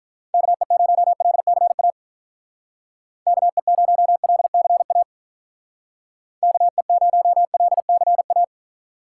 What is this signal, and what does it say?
Morse code, heard as a keyed tone, about 46 wpm, 700 Hz. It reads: KE0LCA KE0LCA KE0LCA